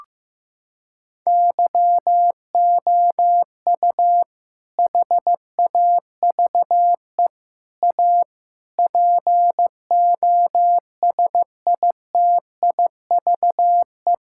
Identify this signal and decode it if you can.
Morse code, heard as a keyed tone, about 16 wpm, 700 Hz. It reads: YOU HAVE A POSITIVE
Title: Sentences from Top 1000 Words - Review Part 5 - 15wpm